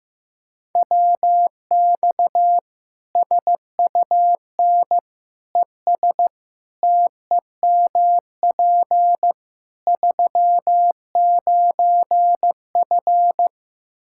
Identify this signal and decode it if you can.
Morse code, heard as a keyed tone, about 15 wpm, 700 Hz. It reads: WX SUN ES TEMP 39F
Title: Practice QSO - Exchange 2 - Info - 15wpm